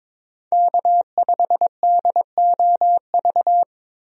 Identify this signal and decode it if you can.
Morse code, heard as a keyed tone, about 22 wpm, 700 Hz. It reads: K5DOV
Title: USA Call Signs - 5 Characters - Repeated 3x - 22wpm